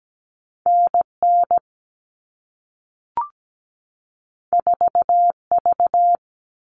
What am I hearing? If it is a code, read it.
Morse code, heard as a keyed tone, about 17 wpm, 700 Hz. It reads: NN 4V
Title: All Cut Numbers - 2-Digit 17wpm